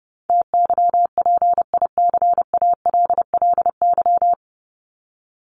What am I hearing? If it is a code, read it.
Morse code, heard as a keyed tone, about 30 wpm, 700 Hz. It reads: TYPICALLY
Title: LY Suffix - Top 100 Words with Base Words 30wpm